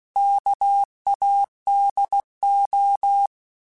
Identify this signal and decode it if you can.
Morse code, heard as a keyed tone, about 16 wpm, 800 Hz. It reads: KADO
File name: Alphabetical / Alphabetical mp3s / Ks @15